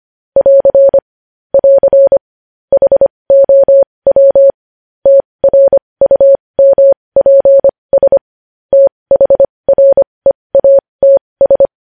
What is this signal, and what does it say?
Morse code, heard as a keyed tone, about 25 wpm, spaced as for 20 wpm, 550 Hz. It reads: <AR> <AR> HOW TRUMPS THREATS
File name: podcast / News headlines-1x-25x20